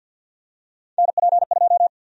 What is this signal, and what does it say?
Morse code, heard as a keyed tone, about 53 wpm, 700 Hz. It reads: NP2
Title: USA Call Sign Prefixes - 50wpm